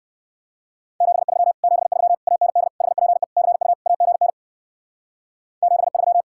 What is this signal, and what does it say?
Morse code, heard as a keyed tone, about 59 wpm, 700 Hz. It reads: 73 73 AND HPE CU AGN 73